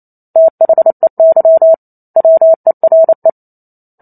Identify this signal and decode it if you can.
Morse code, heard as a keyed tone, about 28 wpm, 650 Hz. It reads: THEY WERE